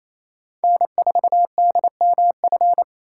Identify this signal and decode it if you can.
Morse code, heard as a keyed tone, about 28 wpm, 700 Hz. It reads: N4DMF
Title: USA Call Signs - 5 Characters - 8x Character Spacing - 28wpm